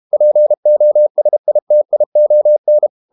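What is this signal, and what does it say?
Morse code, heard as a keyed tone, about 32 wpm, 600 Hz. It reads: POSITION
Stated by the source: Morse code practice at 32 wpm from CW med Gnister, reading Ken Follett's novel Never in English.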